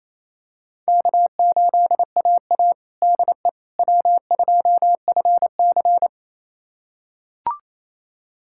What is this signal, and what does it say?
Morse code, heard as a keyed tone, about 28 wpm, 700 Hz. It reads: K8AA DE W2FC
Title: USA Callsign DE Callsign - 4 Characters - 28wpm